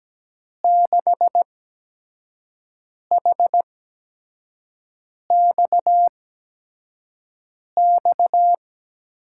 Morse code, heard as a keyed tone, about 17 wpm, spaced as for 5 wpm, 700 Hz. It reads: 6HXX
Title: USA Call Signs - 6 Characters - 8x Character Spacing - 17wpm